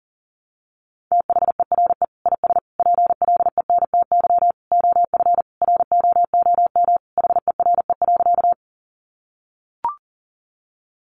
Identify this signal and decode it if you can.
Morse code, heard as a keyed tone, about 40 wpm, 700 Hz. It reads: THERE IS PLENTY OF ROOM HERE.
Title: Sentences from Top 1300 Words 40wpm